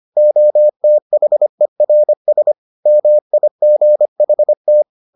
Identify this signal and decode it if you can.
Morse code, heard as a keyed tone, about 25 wpm, 600 Hz. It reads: OTHERS MIGHT